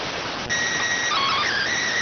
SSTV (slow-scan television), PD 290.